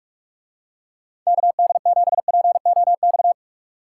Morse code, heard as a keyed tone, about 45 wpm, 700 Hz. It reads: KD7POX